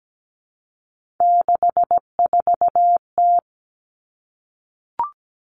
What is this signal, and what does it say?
Morse code, heard as a keyed tone, about 17 wpm, 700 Hz. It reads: 64T
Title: Common Cut Numbers - 3-Digit 17wpm